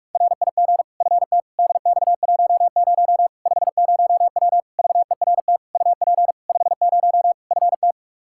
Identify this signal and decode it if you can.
Morse code, heard as a keyed tone, about 46 wpm, 700 Hz. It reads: RIG FT DX10 50W VERT UP 50 FT
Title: Practice QSO - Exchange 2 - Info - 45wpm